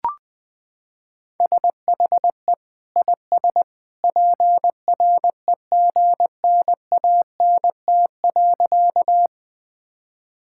Morse code, heard as a keyed tone, about 20 wpm, 700 Hz. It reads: SHE IS PREGNANT.
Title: Sentences from Top 1000 Words - Review Part 6 - 20wpm